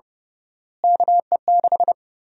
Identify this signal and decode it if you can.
Morse code, heard as a keyed tone, about 30 wpm, 700 Hz. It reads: KE6
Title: USA Call Sign Prefixes - Repeated 2x - 30wpm